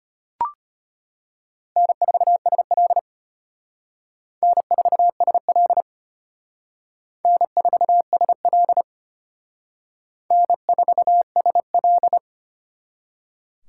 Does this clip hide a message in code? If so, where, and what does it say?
Yes: Morse code from 1.8 to 12.2 s, heard as a keyed tone, about 32 wpm, 700 Hz. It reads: N4SL N4SL N4SL N4SL